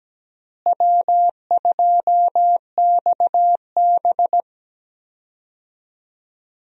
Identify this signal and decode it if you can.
Morse code, heard as a keyed tone, about 17 wpm, 700 Hz. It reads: W2XB
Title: USA Call Signs - 4 Characters - Repeated 3x - 17wpm